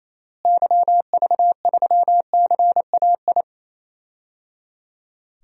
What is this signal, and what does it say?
Morse code, heard as a keyed tone, about 28 wpm, 700 Hz. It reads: YV3CAI
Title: DXCC-100 Call Signs - 6 Characters - Repeated 3x - 28wpm